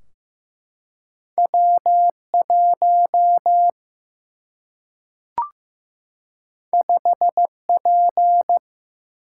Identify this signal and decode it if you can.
Morse code, heard as a keyed tone, about 15 wpm, 700 Hz. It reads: W 1 5 P
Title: Single-Letter Single-Number 15wpm